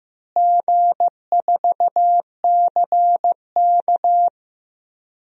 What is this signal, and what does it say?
Morse code, heard as a keyed tone, about 15 wpm, 700 Hz. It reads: G4CK
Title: DXCC-100 Call Signs - 4 Characters - Repeated 3x - 15wpm